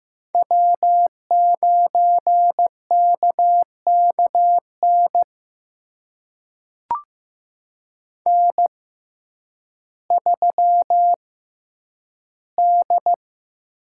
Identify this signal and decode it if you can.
Morse code, heard as a keyed tone, about 15 wpm, spaced as for 9 wpm, 700 Hz. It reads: W9KKN N3D
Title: USA Call Signs - 5 Characters - 6x Character Spacing - 15wpm